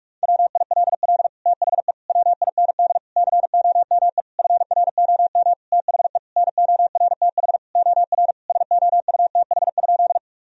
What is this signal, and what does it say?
Morse code, heard as a keyed tone, about 46 wpm, 700 Hz. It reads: WILL THE WIND COME FROM THE NORTH OR SOUTH?
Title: Sentences from Top 1000 Words - Review Part 2 - 45wpm